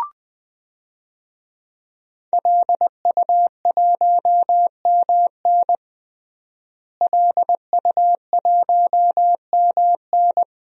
Morse code, heard as a keyed tone, about 21 wpm, 700 Hz. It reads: LU1MN LU1MN
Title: DXCC-100 Call Signs - 5 Characters - Repeated 2x - 20wpm